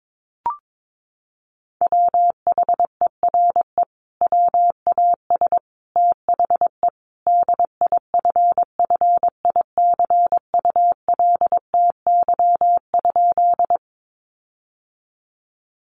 Morse code, heard as a keyed tone, about 22 wpm, 700 Hz. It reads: WHERE WAS THE DIFFICULTY?
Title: Sentences from Top 1500 Words 22wpm